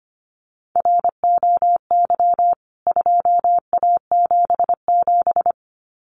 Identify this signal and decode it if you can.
Morse code, heard as a keyed tone, about 25 wpm, 700 Hz. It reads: ROY 2A77
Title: CWT Exchange 25wpm